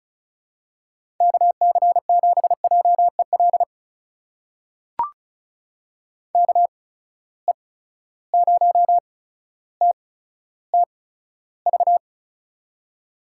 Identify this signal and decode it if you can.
Morse code, heard as a keyed tone, about 36 wpm, spaced as for 15 wpm, 700 Hz. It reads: KC7JEL KE0TTV
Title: USA Call Signs - 6 Characters - 8x Character Spacing - 35wpm